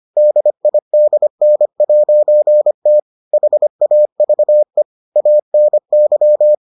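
Morse code, heard as a keyed tone, about 25 wpm, 600 Hz. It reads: DIDN'T HAVE ANY
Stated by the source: Morse edition of the Foundations of Amateur Radio episodes.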